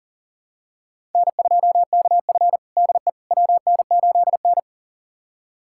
Morse code, heard as a keyed tone, about 37 wpm, 700 Hz. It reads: N2KF DE WN8N
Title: USA Callsign DE Callsign - 4 Characters - 40wpm